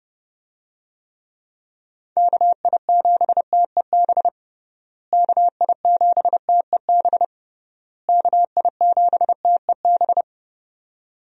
Morse code, heard as a keyed tone, about 30 wpm, 700 Hz. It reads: KI7TEB KI7TEB KI7TEB
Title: USA Call Signs - 6 Characters - Repeated 3x - 30wpm